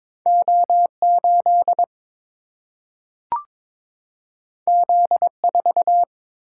Morse code, heard as a keyed tone, about 22 wpm, 700 Hz. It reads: O8 Z4
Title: Single-Letter Single-Number 22wpm